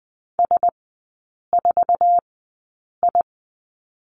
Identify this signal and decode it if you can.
Morse code, heard as a keyed tone, about 20 wpm, spaced as for 8 wpm, 700 Hz. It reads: S4I
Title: 047 Lesson 9e - Number 4 - Three Characters 20wpm